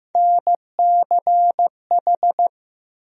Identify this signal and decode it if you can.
Morse code, heard as a keyed tone, about 16 wpm, 700 Hz. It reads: NCH
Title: USA Call Sign Postfixes - 3 Letters 15wpm